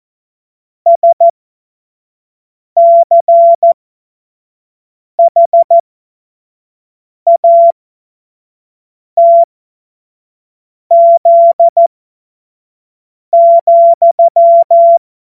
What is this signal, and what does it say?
Morse code, heard as a keyed tone, about 14 wpm, spaced as for 5 wpm, 680 Hz. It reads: SCHATZ,
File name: Morsekurs / Witze / Witz 12. April 2022 25 F70